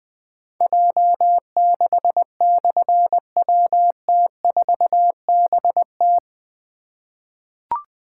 Morse code, heard as a keyed tone, about 20 wpm, 700 Hz. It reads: J6/WT4BT
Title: Difficult Call Signs 20wpm